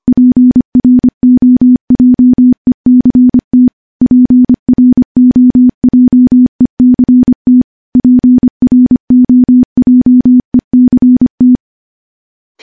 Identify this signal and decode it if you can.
Morse code, heard as a keyed tone, about 25 wpm, 250 Hz. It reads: PROJECT PROJECT PROJECT
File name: morse / morse-training-25WPM